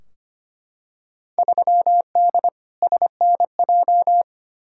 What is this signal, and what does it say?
Morse code, heard as a keyed tone, about 25 wpm, 700 Hz. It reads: 3D SNJ
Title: Field Day Exchanges Encore - 25wpm